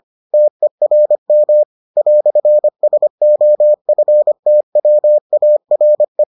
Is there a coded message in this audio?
Yes: Morse code from 0.3 to 6.2 s, heard as a keyed tone, about 25 wpm, 600 Hz. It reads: TERM "SOFTWARE